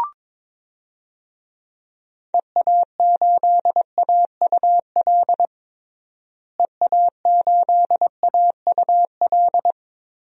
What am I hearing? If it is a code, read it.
Morse code, heard as a keyed tone, about 22 wpm, 700 Hz. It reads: EA8AUL EA8AUL
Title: DXCC-100 Call Signs - 6 Characters - Repeated 3x - 22wpm